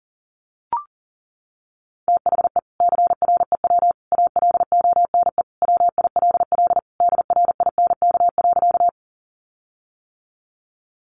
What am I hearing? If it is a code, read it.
Morse code, heard as a keyed tone, about 40 wpm, 700 Hz. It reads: THE CREW ALONE WILL DRINK.
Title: Sentences from Top 1500 Words 40wpm